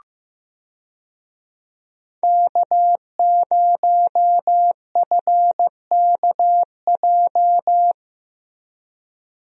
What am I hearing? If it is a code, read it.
Morse code, heard as a keyed tone, about 15 wpm, 700 Hz. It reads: K0FKJ
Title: USA Call Signs - 5 Characters - Repeated 2x - 15wpm